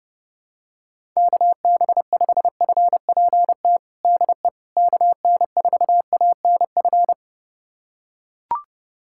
Morse code, heard as a keyed tone, about 30 wpm, 700 Hz. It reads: KB5FPT DE KN4ANF
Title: USA Callsign DE Callsign - 6 Characters - 30wpm